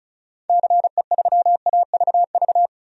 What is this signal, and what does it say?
Morse code, heard as a keyed tone, about 36 wpm, 700 Hz. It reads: CE3AVV
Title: DXCC-100 Call Signs - 6 Characters - 35wpm